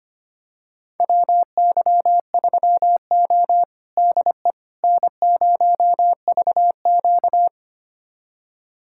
Morse code, heard as a keyed tone, about 25 wpm, 700 Hz. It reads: WY3O DE N0VQ
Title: USA Callsign DE Callsign - 4 Characters - 25wpm